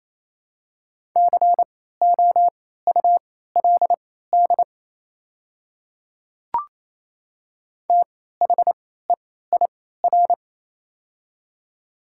Morse code, heard as a keyed tone, about 28 wpm, spaced as for 16 wpm, 700 Hz. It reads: COULD THEIR
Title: 3x Character Spacing - Top 200 Words - 28wpm